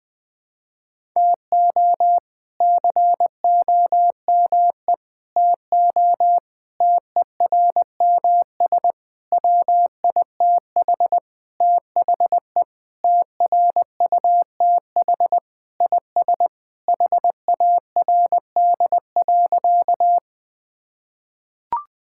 Morse code, heard as a keyed tone, about 21 wpm, 700 Hz. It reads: TO COME TO TERMS WITH THE TRUTH IS HARD.
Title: Sentences with 6-Word Phrases - 20wpm